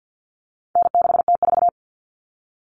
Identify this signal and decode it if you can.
Morse code, heard as a keyed tone, about 50 wpm, 700 Hz. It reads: N6T4
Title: All Cut Numbers - 4-Digit 50wpm